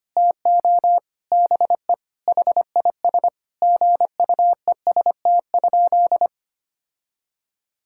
Morse code, heard as a keyed tone, about 25 wpm, 700 Hz. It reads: TO BE HIS GUEST?
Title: Sentences from Top 1600 Words 25wpm